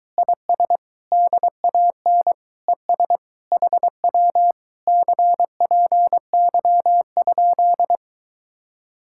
Morse code, heard as a keyed tone, about 22 wpm, 700 Hz. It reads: IS DAN ES HW CPY?